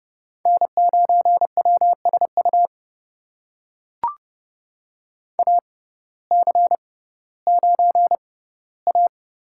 Morse code, heard as a keyed tone, about 30 wpm, spaced as for 17 wpm, 700 Hz. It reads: N9WSU AC9A
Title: USA Call Signs - 6x Character Spacing - 30wpm